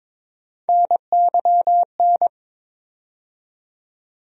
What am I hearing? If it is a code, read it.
Morse code, heard as a keyed tone, about 22 wpm, 700 Hz. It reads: NYN
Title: USA Call Sign Postfixes - 3 Letters 22wpm